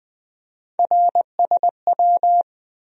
Morse code, heard as a keyed tone, about 20 wpm, 700 Hz. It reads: RSW
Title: USA Call Sign Postfixes - 3 Letters 20wpm